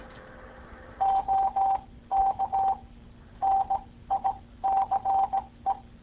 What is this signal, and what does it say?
Morse code, heard as a keyed tone, about 18 wpm, spaced as for 14 wpm, 780 Hz. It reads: OK NICE